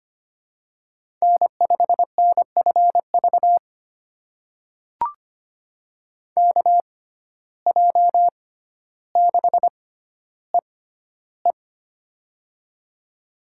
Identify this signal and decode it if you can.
Morse code, heard as a keyed tone, about 25 wpm, spaced as for 13 wpm, 700 Hz. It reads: N5NFV KJ6EE
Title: USA Call Signs - 5 Characters - 6x Character Spacing - 25wpm